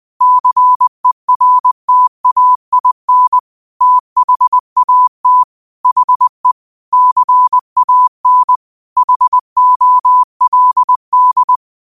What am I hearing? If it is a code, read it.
Morse code, heard as a keyed tone, about 20 wpm, 1000 Hz. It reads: CERTAIN THAT HE CAN HOLD